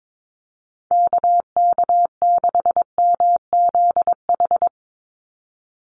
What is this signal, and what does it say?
Morse code, heard as a keyed tone, about 22 wpm, 700 Hz. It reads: KK6MZH
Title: USA Call Signs 22wpm